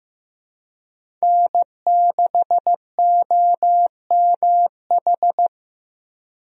Morse code, heard as a keyed tone, about 15 wpm, 700 Hz. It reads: N6OMH
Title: USA Call Signs - 5 Characters - Repeated 3x - 15wpm